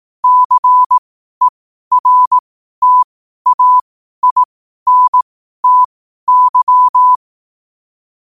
Morse code, heard as a keyed tone, about 18 wpm, spaced as for 12 wpm, 1000 Hz. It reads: CERTAINTY
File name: qotdfiles / 12wpm